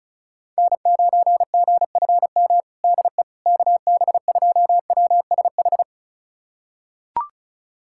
Morse code, heard as a keyed tone, about 34 wpm, 700 Hz. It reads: N9GFM DE KB2WSH
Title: USA Callsign DE Callsign 35wpm